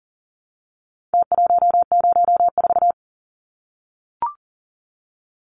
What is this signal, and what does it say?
Morse code, heard as a keyed tone, about 40 wpm, 700 Hz. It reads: T104
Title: Numbers - Contest Style - 4-Digit 40wpm